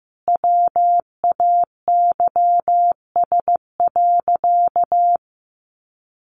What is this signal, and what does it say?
Morse code, heard as a keyed tone, about 15 wpm, 700 Hz. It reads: WAYS.
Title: A Sound Of Thunder 15wpm